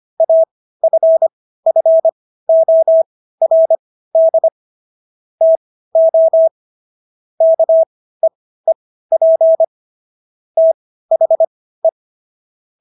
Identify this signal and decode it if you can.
Morse code, heard as a keyed tone, about 25 wpm, spaced as for 15 wpm, 650 Hz. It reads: AFFORD TO KEEP THE